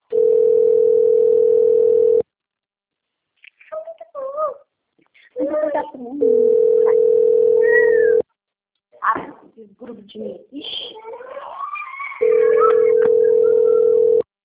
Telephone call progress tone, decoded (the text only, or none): ringback tone